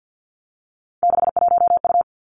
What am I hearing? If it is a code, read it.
Morse code, heard as a keyed tone, about 50 wpm, 700 Hz. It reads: 61U
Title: Three Challenging Letter-Number Combinations 50wpm